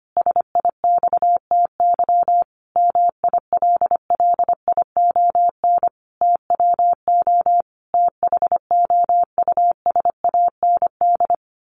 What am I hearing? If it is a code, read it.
Morse code, heard as a keyed tone, about 25 wpm, 700 Hz. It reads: SIXTY MILLION TWO THOUSAND
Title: A Sound Of Thunder 25wpm